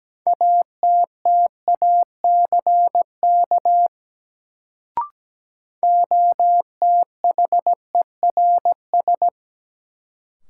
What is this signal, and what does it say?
Morse code, heard as a keyed tone, about 17 wpm, 700 Hz. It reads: ATTACK OTHERS
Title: Sentences from Top 1500 Words - New Word Drill Alt - 17wpm